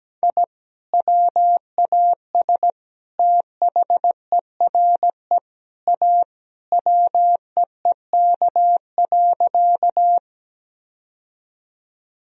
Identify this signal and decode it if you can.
Morse code, heard as a keyed tone, about 17 wpm, 700 Hz. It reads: I WAS THERE A WEEK.
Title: Sentences from Top 1000 Words - Review Part 2 - 17wpm